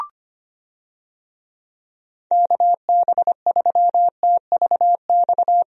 Morse code, heard as a keyed tone, about 25 wpm, 700 Hz. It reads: KB3TVX
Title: USA Call Signs - 6 Characters - 25wpm